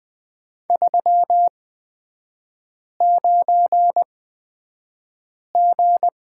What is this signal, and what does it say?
Morse code, heard as a keyed tone, about 20 wpm, spaced as for 5 wpm, 690 Hz. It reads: 39G
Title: 229 Lesson 37e - Letter x - Three Characters 20wpm